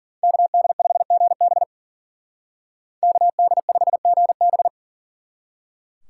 Morse code, heard as a keyed tone, about 48 wpm, 700 Hz. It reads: KD5GB KD5GB
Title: Speed Racing - USA Call Signs 40wpm